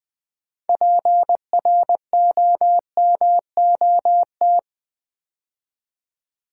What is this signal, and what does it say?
Morse code, heard as a keyed tone, about 20 wpm, 700 Hz. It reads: PROMOT
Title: Build A Word - 3 Letters At A Time - 9-Letter Words - 20wpm